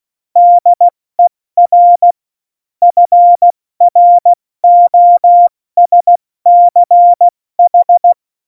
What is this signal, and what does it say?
Morse code, heard as a keyed tone, about 16 wpm, 700 Hz. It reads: DER FROSCH
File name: Morsekurs / Witze / Witz 11. September 2018, 70 F80